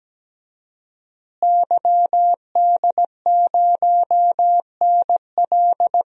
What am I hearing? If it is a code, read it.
Morse code, heard as a keyed tone, about 17 wpm, 700 Hz. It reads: YD0NL
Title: DXCC-100 Call Signs - 5 Characters - Repeated 2x - 17wpm